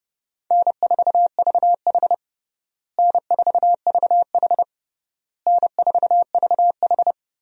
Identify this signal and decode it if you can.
Morse code, heard as a keyed tone, about 32 wpm, 700 Hz. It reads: N4VH N4VH N4VH
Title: USA Call Signs - 4 Characters - Repeated 3x - 30wpm